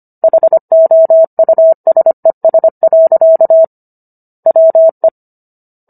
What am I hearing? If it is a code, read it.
Morse code, heard as a keyed tone, about 25 wpm, 650 Hz. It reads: HOUSES. WE